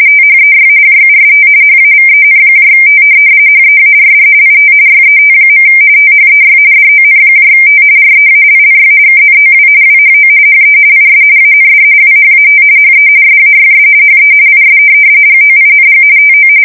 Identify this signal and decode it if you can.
RTTY, about 45.45 baud. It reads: THEY JUST HAVENT HEARD IT YET. THEY DONT REALLY KNOW WHO HE IS AND WHAT HE SAYS HE IS GOING TO
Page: 170 Shift 2125/2295